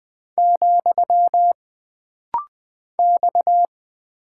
Morse code, heard as a keyed tone, about 20 wpm, 700 Hz. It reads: , X
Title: Single Character (Letter:Number:Punctuation:Prosign) - Rapid-Fire - 20wpm